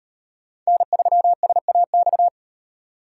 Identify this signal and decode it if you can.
Morse code, heard as a keyed tone, about 38 wpm, 700 Hz. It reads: N3SAX